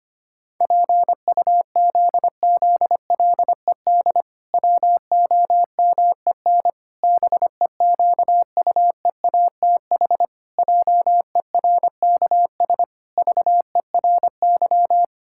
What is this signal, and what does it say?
Morse code, heard as a keyed tone, about 25 wpm, 700 Hz. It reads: PUZZLED WOMEN BEQUEATH JERKS VERY
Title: Pangram Challenge v2 - 25wpm